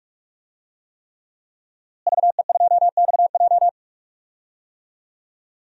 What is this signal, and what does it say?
Morse code, heard as a keyed tone, about 41 wpm, 700 Hz. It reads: VE2XJ
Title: DXCC-100 Call Signs - 5 Characters - Repeated 3x - 45wpm